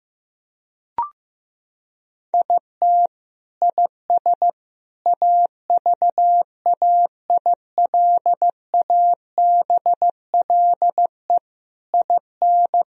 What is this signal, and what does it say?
Morse code, heard as a keyed tone, about 15 wpm, 700 Hz. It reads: IT IS AVAILABLE IN
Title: Sentences from Top 1000 Words - Review Part 3 - 15wpm